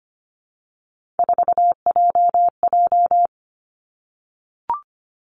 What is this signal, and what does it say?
Morse code, heard as a keyed tone, about 25 wpm, 700 Hz. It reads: 4JJ
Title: Three Challenging Letter-Number Combinations 25wpm